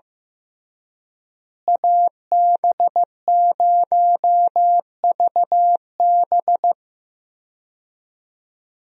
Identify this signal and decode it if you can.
Morse code, heard as a keyed tone, about 15 wpm, 700 Hz. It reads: AB0VB
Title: USA Call Signs - 5 Characters - Repeated 2x - 15wpm